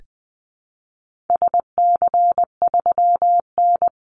Morse code, heard as a keyed tone, about 20 wpm, 700 Hz. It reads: SC3N
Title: Call Signs - 20wpm